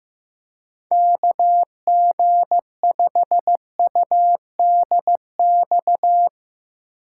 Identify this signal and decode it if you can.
Morse code, heard as a keyed tone, about 15 wpm, 700 Hz. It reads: KG5UDX
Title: USA Call Signs - 6 Characters - 6x Character Spacing - 15wpm